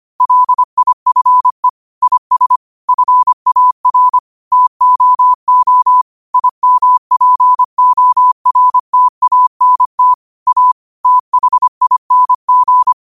Morse code, heard as a keyed tone, about 25 wpm, 1000 Hz. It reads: LIFE IS FAR TOO IMPORTANT A THING
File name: qotdfiles / 25wpm